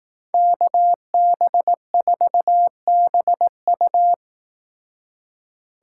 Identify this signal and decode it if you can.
Morse code, heard as a keyed tone, about 19 wpm, 700 Hz. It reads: KB4BU
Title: Speed Racing - USA 2x2 Call Signs 15wpm